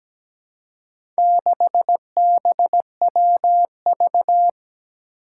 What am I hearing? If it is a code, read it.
Morse code, heard as a keyed tone, about 17 wpm, 700 Hz. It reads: 6BWV
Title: Four Challenging Letter-Number Combinations 17wpm